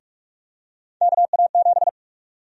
Morse code, heard as a keyed tone, about 47 wpm, 700 Hz. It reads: KA7
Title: USA Call Sign Prefixes - Repeated 2x - 45wpm